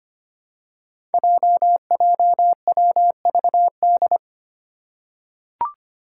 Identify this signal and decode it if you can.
Morse code, heard as a keyed tone, about 25 wpm, 700 Hz. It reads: JJWVD
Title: Code Groups - 5 Letters - 25wpm